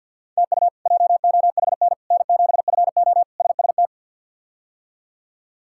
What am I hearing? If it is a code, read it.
Morse code, heard as a keyed tone, about 52 wpm, 700 Hz. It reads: TU JOHN N7FO SST
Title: K1USN SST QSO Practice - 50wpm